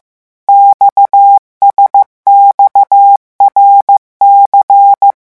Morse code, heard as a keyed tone, about 15 wpm, 800 Hz. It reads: XSXRC
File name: morse mp3 / revision11 lettres vitesse 14